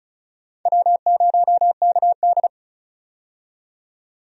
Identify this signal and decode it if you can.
Morse code, heard as a keyed tone, about 36 wpm, 700 Hz. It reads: W0KD
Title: USA Call Signs - 4 Characters - 35wpm